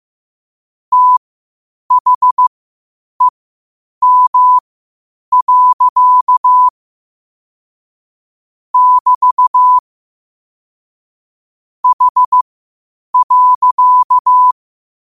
Morse code, heard as a keyed tone, about 15 wpm, spaced as for 8 wpm, 1000 Hz. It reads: THEM. <BT> H.
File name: qotdfiles / 8wpm